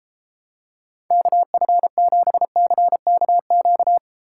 Morse code, heard as a keyed tone, about 29 wpm, 700 Hz. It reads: KF7CKQ